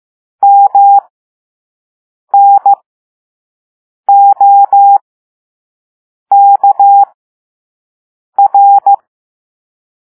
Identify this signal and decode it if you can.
Morse code, heard as a keyed tone, about 15 wpm, spaced as for 5 wpm, 800 Hz. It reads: MNOKR